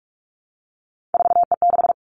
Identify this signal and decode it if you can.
Morse code, heard as a keyed tone, about 46 wpm, 700 Hz. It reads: 4E6
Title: Three Challenging Letter-Number Combinations 45wpm